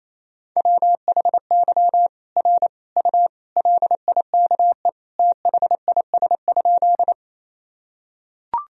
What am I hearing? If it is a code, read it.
Morse code, heard as a keyed tone, about 28 wpm, 700 Hz. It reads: WHY R U LIKE THIS?
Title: Sentences from Top 200 Words and Abbreviations 28wpm